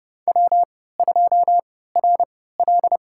Morse code, heard as a keyed tone, about 30 wpm, spaced as for 17 wpm, 700 Hz. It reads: W2RL
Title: USA Call Signs - 4 Characters - 3x Character Spacing - 30wpm